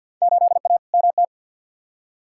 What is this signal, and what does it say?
Morse code, heard as a keyed tone, about 50 wpm, 700 Hz. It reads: 8A MT
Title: Field Day Exchanges Encore - 50wpm